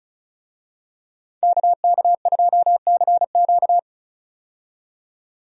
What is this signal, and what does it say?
Morse code, heard as a keyed tone, about 36 wpm, 700 Hz. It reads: KK2CQ
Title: USA Call Signs - 5 Characters - 35wpm